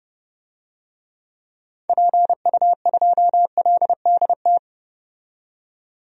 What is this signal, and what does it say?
Morse code, heard as a keyed tone, about 30 wpm, 700 Hz. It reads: PU2LDT
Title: DXCC-100 Call Signs - 6 Characters - 30wpm